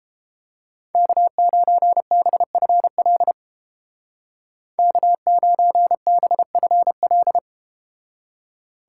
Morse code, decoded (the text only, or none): K9BFL K9BFL